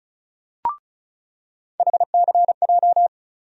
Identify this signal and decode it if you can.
Morse code, heard as a keyed tone, about 36 wpm, 700 Hz. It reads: HCJ
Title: USA Call Sign Postfixes - 3 Letters 35wpm